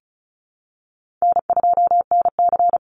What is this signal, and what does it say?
Morse code, heard as a keyed tone, about 34 wpm, 700 Hz. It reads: N2NC
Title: Call Signs Repeated 3x - 35wpm